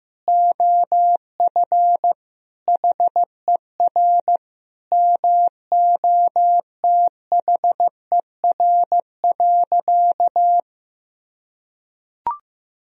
Morse code, heard as a keyed tone, about 15 wpm, 700 Hz. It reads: OF HER MOTHER.
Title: Sentences from Top 1000 Words - Review Part 4 - 15wpm